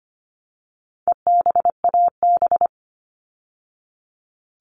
Morse code, heard as a keyed tone, about 25 wpm, 700 Hz. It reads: EBAB
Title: All Cut Numbers - 4-Digit 25wpm